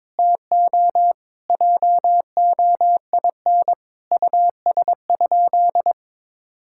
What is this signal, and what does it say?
Morse code, heard as a keyed tone, about 22 wpm, 700 Hz. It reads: TO JOIN US?
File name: practice / 22wpm / Sentences with 4-Word Phrases - 22wpm